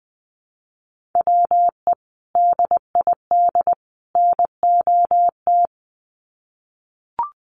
Morse code, heard as a keyed tone, about 20 wpm, 700 Hz. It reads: WE DID NOT
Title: Sets of 3 Words 20wpm